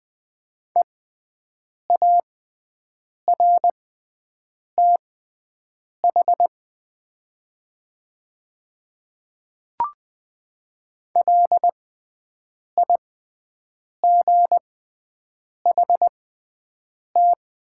Morse code, heard as a keyed tone, about 20 wpm, spaced as for 7 wpm, 700 Hz. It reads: EARTH LIGHT
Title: 6x Character Spacing - Top 200 Words - 20wpm